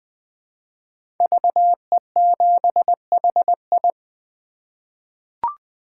Morse code, heard as a keyed tone, about 20 wpm, 700 Hz. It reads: VE7HI
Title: 2023 SOTA Chaser Call Signs - 20wpm